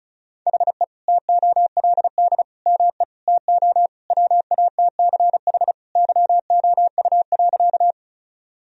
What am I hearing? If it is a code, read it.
Morse code, heard as a keyed tone, about 36 wpm, 700 Hz. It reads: HE TOLD ME TO WATCH YOU.
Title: Sentences from Top 1000 Words - Review Part 1 - 35wpm